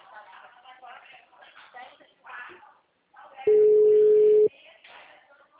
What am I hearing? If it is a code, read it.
Ringback tone (Europe, 425 Hz), with speech.